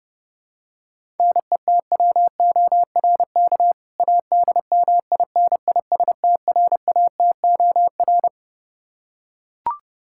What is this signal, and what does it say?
Morse code, heard as a keyed tone, about 30 wpm, 700 Hz. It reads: NETWORK ADMINISTRATOR
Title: K1USN SST QSO Practice 30wpm